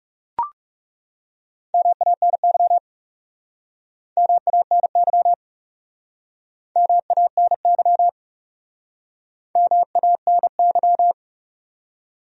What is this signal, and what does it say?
Morse code, heard as a keyed tone, about 37 wpm, 700 Hz. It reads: MANY MANY MANY MANY
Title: Speed-Racing Top 100 Words 30wpm